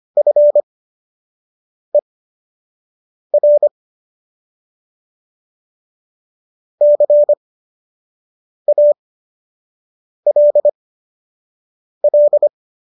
Morse code, heard as a keyed tone, about 25 wpm, spaced as for 6 wpm, 590 Hz. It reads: FER CALL